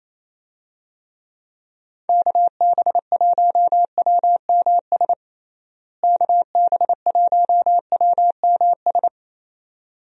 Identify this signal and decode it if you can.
Morse code, heard as a keyed tone, about 28 wpm, 700 Hz. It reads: KB1WMS KB1WMS
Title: USA Call Signs - 6 Characters - Repeated 2x - 28wpm